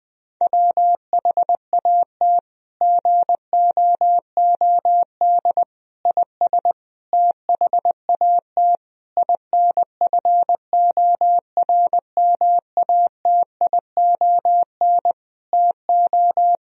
Morse code, heard as a keyed tone, about 21 wpm, 700 Hz. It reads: WHAT GOOD IS THAT INFORMATION TO
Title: Sentences from Top 1000 Words - Review Part 3 - 20wpm